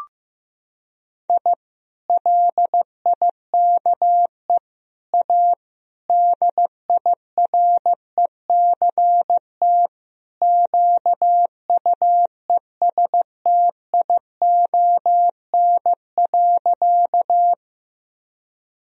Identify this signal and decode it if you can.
Morse code, heard as a keyed tone, about 16 wpm, 700 Hz. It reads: I LIKE A DIRECT QUESTION.
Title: Sentences from Top 1000 Words - Review Part 2 - 15wpm